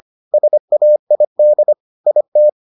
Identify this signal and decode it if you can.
Morse code, heard as a keyed tone, about 25 wpm, 600 Hz. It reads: SAID IT